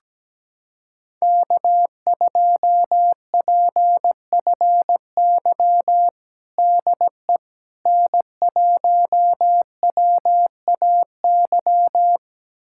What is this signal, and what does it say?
Morse code, heard as a keyed tone, about 17 wpm, 700 Hz. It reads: K2PFY DE N1WAY
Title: USA Callsign DE Callsign 17wpm